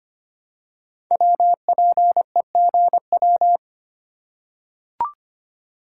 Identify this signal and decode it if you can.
Morse code, heard as a keyed tone, about 25 wpm, 700 Hz. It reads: WPEGW
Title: Code Groups - 5 Letter:Numbers - 25wpm